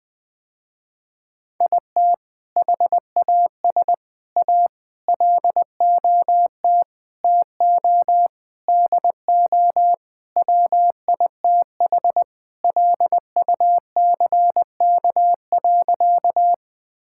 Morse code, heard as a keyed tone, about 20 wpm, 700 Hz. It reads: IT HAS A LOT TO DO WITH LUCK.
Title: Sentences with 5-Word Phrases - Spoken Beginning - 20wpm